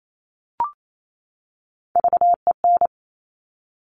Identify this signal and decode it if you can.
Morse code, heard as a keyed tone, about 28 wpm, 700 Hz. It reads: VEN
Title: Top Three Letter Combinations in Words 28wpm